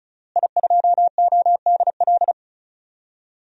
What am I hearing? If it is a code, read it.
Morse code, heard as a keyed tone, about 34 wpm, 700 Hz. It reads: I2ODL
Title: DXCC-100 Call Signs - 5 Characters - Repeated 3x - 35wpm